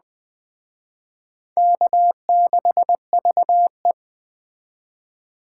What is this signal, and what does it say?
Morse code, heard as a keyed tone, about 20 wpm, 700 Hz. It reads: K6VE
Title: Speed Racing - USA 1x2 Call Signs 17wpm